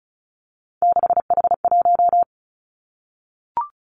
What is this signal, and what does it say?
Morse code, heard as a keyed tone, about 34 wpm, 700 Hz. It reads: 6H1
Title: Three Challenging Letter-Number Combinations 35wpm